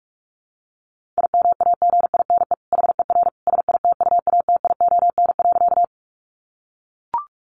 Morse code, heard as a keyed tone, about 44 wpm, 700 Hz. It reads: IMAGINE HER SITUATION.
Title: Sentences from Top 1400 Words 45wpm